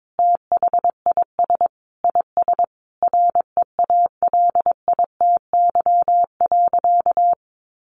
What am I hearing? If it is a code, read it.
Morse code, heard as a keyed tone, about 22 wpm, 700 Hz. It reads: THIS IS REALITY.
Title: Sentences from Top 1200 Words 22wpm